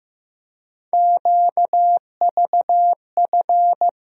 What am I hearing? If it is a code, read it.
Morse code, heard as a keyed tone, about 15 wpm, 700 Hz. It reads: QVF
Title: USA Call Sign Postfixes - 3 Letters 15wpm